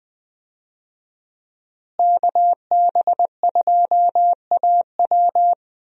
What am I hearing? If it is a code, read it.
Morse code, heard as a keyed tone, about 20 wpm, 700 Hz. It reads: KB2AW
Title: USA Call Signs - 5 Characters - Repeated 3x - 20wpm